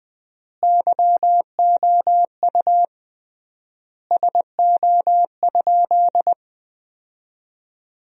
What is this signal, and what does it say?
Morse code, heard as a keyed tone, about 20 wpm, 700 Hz. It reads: YOU SO?
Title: 3x Word-Spacing - Sentences from Top 400 Words - 20wpm